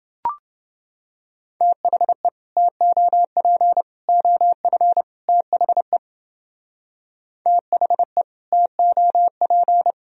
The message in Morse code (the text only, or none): THE TOP OF THE THE TOP